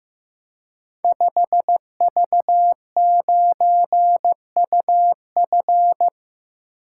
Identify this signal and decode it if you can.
Morse code, heard as a keyed tone, about 15 wpm, 700 Hz. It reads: 5V9UF
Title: Code Groups - 5 Letter:Numbers - 15wpm